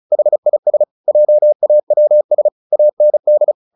Morse code, heard as a keyed tone, about 34 wpm, 600 Hz. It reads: HIS JAWS AND
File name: the old man and the sea 2 35wpm